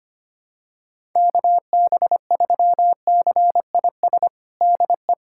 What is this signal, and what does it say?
Morse code, heard as a keyed tone, about 25 wpm, 700 Hz. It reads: KB3CIS DE
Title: USA Callsign DE Callsign - 6 Characters - 25wpm